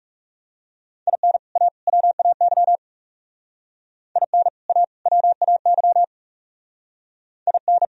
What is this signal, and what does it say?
Morse code, heard as a keyed tone, about 34 wpm, 700 Hz. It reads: IN A WAY IN A WAY IN
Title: Speed-Racing - Sets of 3 Words 30wpm